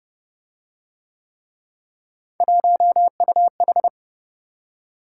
Morse code, heard as a keyed tone, about 26 wpm, 700 Hz. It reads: 1UH